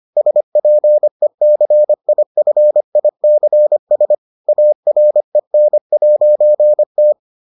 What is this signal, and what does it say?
Morse code, heard as a keyed tone, about 25 wpm, 600 Hz. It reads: SPECIFICS AREN'T